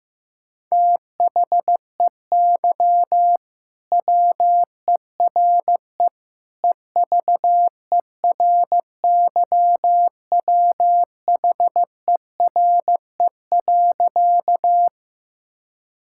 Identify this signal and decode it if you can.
Morse code, heard as a keyed tone, about 15 wpm, 700 Hz. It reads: THEY WERE EVERYWHERE.
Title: Sentences from Top 1600 Words 15wpm